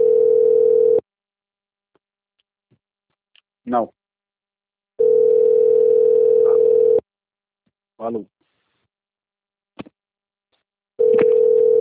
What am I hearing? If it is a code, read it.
Ringback tone (North America), 440 + 480 Hz, with speech.
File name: out- - - - - .18149